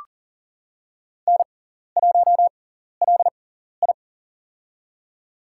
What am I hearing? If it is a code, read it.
Morse code, heard as a keyed tone, about 40 wpm, spaced as for 14 wpm, 700 Hz. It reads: N1LI
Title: USA Call Signs - 4 Characters - 6x Character Spacing - 40wpm